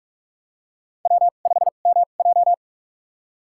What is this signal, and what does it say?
Morse code, heard as a keyed tone, about 45 wpm, spaced as for 32 wpm, 700 Hz. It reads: W5MJ
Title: USA Call Signs - 4 Characters - 2x Character Spacing - 45wpm